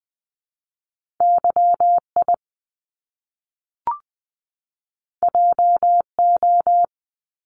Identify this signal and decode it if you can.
Morse code, heard as a keyed tone, about 20 wpm, 700 Hz. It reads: YI JO
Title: Top Two Letter Combinations in Words 20wpm